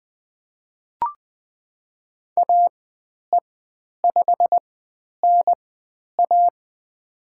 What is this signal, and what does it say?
Morse code, heard as a keyed tone, about 20 wpm, spaced as for 10 wpm, 700 Hz. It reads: AE5NA
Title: 109 Lesson 18g - Letter m - Call Signs 20wpm